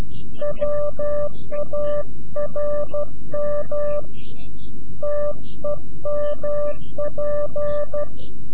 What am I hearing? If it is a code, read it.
Morse code, heard as a keyed tone, about 12 wpm, 590 Hz. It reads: WARM TEMP
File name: CW-QSO / CQ Call / CQ-QSO